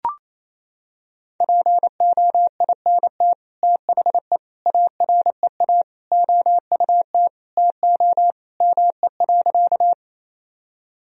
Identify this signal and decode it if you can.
Morse code, heard as a keyed tone, about 27 wpm, 700 Hz. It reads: POINT THE AREA OUT TO ME.
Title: Sentences from Top 1000 Words - Review Part 2 - 28wpm